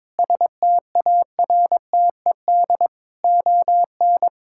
Morse code, heard as a keyed tone, about 22 wpm, 700 Hz. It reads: STARTED ON
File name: practice / 22wpm / Sets of 8 Words - 22wpm